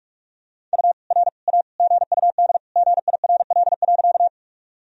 Morse code, heard as a keyed tone, about 43 wpm, 700 Hz. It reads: U R A GUD GIRL.
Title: Sentences from Top 400 Words and Abbreviations 45wpm